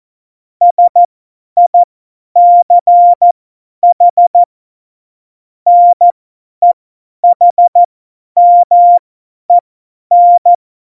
Morse code, heard as a keyed tone, about 14 wpm, spaced as for 10 wpm, 700 Hz. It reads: SICH NEHMEN
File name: Morsekurs / Witze / Witz 11. Februar 2020, 50 F70